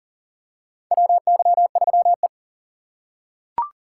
Morse code, heard as a keyed tone, about 40 wpm, 700 Hz. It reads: WY3E
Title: USA Call Signs - 4 Characters - 3x Character Spacing - 40wpm